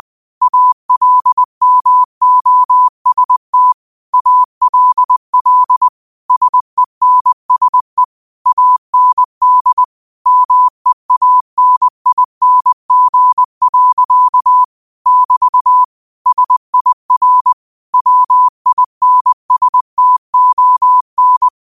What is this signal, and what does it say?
Morse code, heard as a keyed tone, about 20 wpm, 1000 Hz. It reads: ALMOST ALL SENSE AND MEANING. <BT> SIR WINSTON
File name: qotdfiles / 20wpm